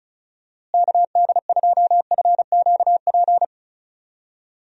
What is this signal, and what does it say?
Morse code, heard as a keyed tone, about 36 wpm, 700 Hz. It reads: KD2FQP
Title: USA Call Signs - 6 Characters - Repeated 3x - 35wpm